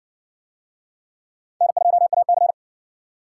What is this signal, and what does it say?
Morse code, heard as a keyed tone, about 58 wpm, 700 Hz. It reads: N2AC